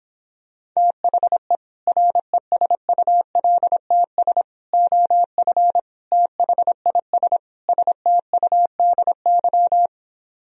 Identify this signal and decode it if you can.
Morse code, heard as a keyed tone, about 26 wpm, 700 Hz. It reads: THE RESULTS OF THIS STUDY